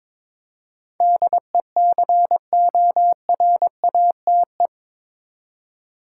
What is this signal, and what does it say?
Morse code, heard as a keyed tone, about 22 wpm, 700 Hz. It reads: DECORATE
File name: practice / 22wpm / Eight Letter Words - 22wpm